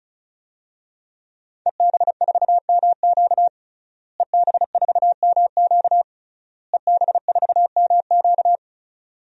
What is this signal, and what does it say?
Morse code, heard as a keyed tone, about 36 wpm, 700 Hz. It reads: EB4MQ EB4MQ EB4MQ
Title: DXCC-100 Call Signs - 5 Characters - Repeated 3x - 35wpm